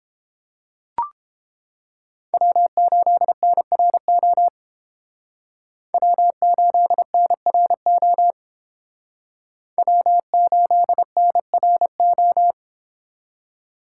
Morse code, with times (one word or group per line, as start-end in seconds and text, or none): 2.3-4.5 W8NRO
5.9-8.3 W8NRO
9.8-12.5 W8NRO